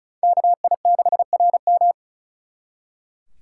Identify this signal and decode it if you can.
Morse code, heard as a keyed tone, about 46 wpm, 700 Hz. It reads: KI6RM